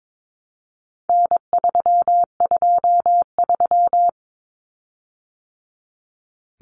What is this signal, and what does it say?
Morse code, heard as a keyed tone, about 22 wpm, 700 Hz. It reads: N323
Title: Common Cut Numbers - 4-Digit 22wpm